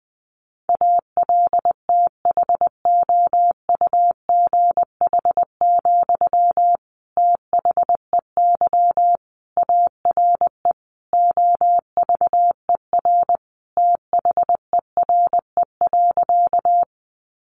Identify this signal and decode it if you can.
Morse code, heard as a keyed tone, about 20 wpm, 700 Hz. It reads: ALTHOUGH, THEY ARE OVER THERE.
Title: Sentences from Top 600 Words 20wpm